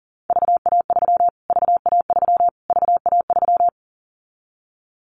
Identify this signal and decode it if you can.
Morse code, heard as a keyed tone, about 40 wpm, 700 Hz. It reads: VA3 VA3 VA3
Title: Callsign Prefixes for Top 25 DXCC - Repeated 3x - 40wpm